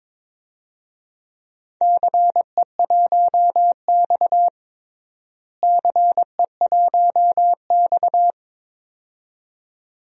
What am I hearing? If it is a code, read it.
Morse code, heard as a keyed tone, about 21 wpm, 700 Hz. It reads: CE1X CE1X
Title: DXCC-100 Call Signs - 4 Characters - Repeated 2x - 22wpm